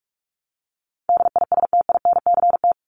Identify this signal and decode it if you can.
Morse code, heard as a keyed tone, about 44 wpm, 700 Hz. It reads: DISTINCT
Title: Top 1501-1750 Words 45wpm